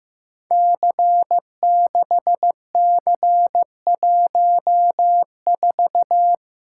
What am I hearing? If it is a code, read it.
Morse code, heard as a keyed tone, about 15 wpm, 700 Hz. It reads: C6C14
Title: Code Groups - 5 Letter:Numbers - 15wpm